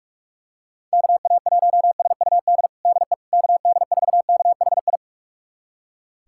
Morse code, heard as a keyed tone, about 46 wpm, 690 Hz. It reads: KA1SUD DE KD4KHI
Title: USA Callsign DE Callsign - 6 Characters - 45wpm